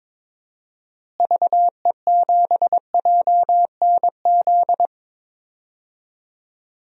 Morse code, heard as a keyed tone, about 22 wpm, 700 Hz. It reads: VE7JNZ
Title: DXCC-100 Call Signs - 6 Characters - Repeated 2x - 22wpm